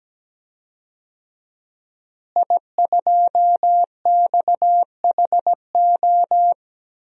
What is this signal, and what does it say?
Morse code, heard as a keyed tone, about 17 wpm, 700 Hz. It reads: I2XHO
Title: DXCC-100 Call Signs - 5 Characters - Repeated 3x - 17wpm